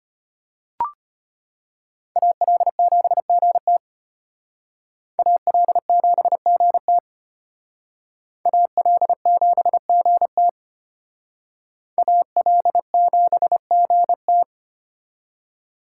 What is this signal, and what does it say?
Morse code, heard as a keyed tone, about 32 wpm, 700 Hz. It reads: AL7GT AL7GT AL7GT AL7GT